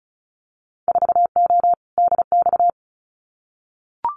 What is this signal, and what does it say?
Morse code, heard as a keyed tone, about 34 wpm, 700 Hz. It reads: 4O DX
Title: Winter Field Day Exchanges Encore 35wpm